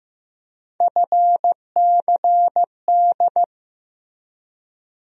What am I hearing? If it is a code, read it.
Morse code, heard as a keyed tone, about 15 wpm, 700 Hz. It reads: FCD
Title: USA Call Sign Postfixes - 3 Letters 15wpm